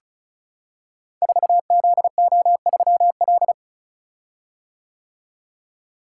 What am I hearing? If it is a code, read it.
Morse code, heard as a keyed tone, about 36 wpm, 690 Hz. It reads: 4ZO3L
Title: Code Groups - 5 Letter:Numbers - 35wpm